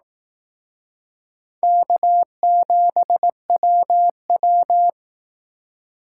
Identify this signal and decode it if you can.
Morse code, heard as a keyed tone, about 19 wpm, 700 Hz. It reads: K7WW